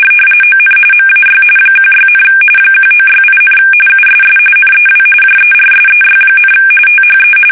RTTY, about 45.45 baud. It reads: ON MAY 16, 1968, USS NEW JERSEY, THE
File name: fsk850 / battleshipnj